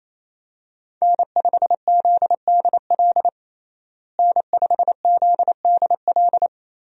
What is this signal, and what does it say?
Morse code, heard as a keyed tone, about 28 wpm, 700 Hz. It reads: N5ZDL N5ZDL
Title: USA Call Signs - 5 Characters - Repeated 3x - 28wpm